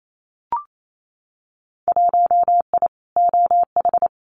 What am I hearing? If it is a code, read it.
Morse code, heard as a keyed tone, about 28 wpm, 700 Hz. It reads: 1I OH
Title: Winter Field Day Exchanges 28wpm